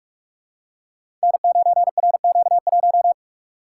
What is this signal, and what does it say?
Morse code, heard as a keyed tone, about 44 wpm, 700 Hz. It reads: N9RQ1
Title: Code Groups - 5 Letter:Numbers - 45wpm